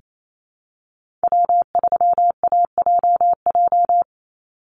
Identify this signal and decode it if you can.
Morse code, heard as a keyed tone, about 28 wpm, 700 Hz. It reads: W3AJJ
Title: USA Call Signs 28wpm